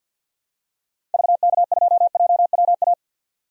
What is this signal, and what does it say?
Morse code, heard as a keyed tone, about 51 wpm, 700 Hz. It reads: VK2JWA
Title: DXCC-100 Call Signs - 6 Characters - 50wpm